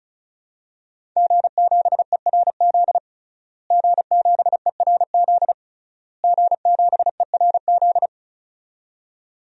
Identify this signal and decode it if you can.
Morse code, heard as a keyed tone, about 34 wpm, 700 Hz. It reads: G7ERZ G7ERZ G7ERZ
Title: DXCC-100 Call Signs - 5 Characters - Repeated 3x - 35wpm